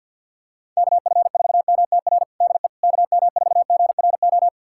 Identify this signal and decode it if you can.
Morse code, heard as a keyed tone, about 50 wpm, 700 Hz. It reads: KF4MTR DE KM4GRO
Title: USA Callsign DE Callsign 50wpm